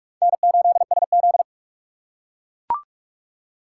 Morse code, heard as a keyed tone, about 44 wpm, 700 Hz. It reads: N8SZ
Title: USA Call Signs - 4 Characters - 4x Character Spacing - 45wpm